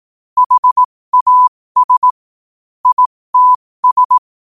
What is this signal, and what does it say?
Morse code, heard as a keyed tone, about 18 wpm, spaced as for 15 wpm, 1000 Hz. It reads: HAS ITS